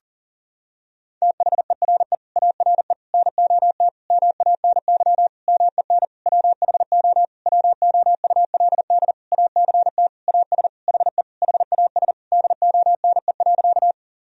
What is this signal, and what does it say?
Morse code, heard as a keyed tone, about 40 wpm, 700 Hz. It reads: THERE ARE NOT MANY MEN WHO WOULD ACT AS HE HAS DONE.
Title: Sentences from Top 1000 Words - Review Part 1 - 40wpm